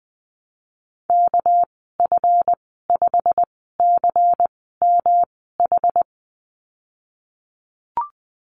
Morse code, heard as a keyed tone, about 19 wpm, spaced as for 14 wpm, 700 Hz. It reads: KF5CMH
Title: 238 Lesson 38g - Letter q - Call Signs 20wpm